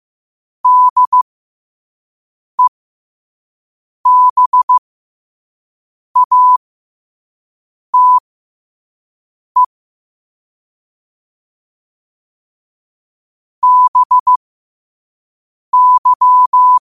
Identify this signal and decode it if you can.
Morse code, heard as a keyed tone, about 15 wpm, spaced as for 5 wpm, 1000 Hz. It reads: DEBATE BY
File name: qotdfiles / 5wpm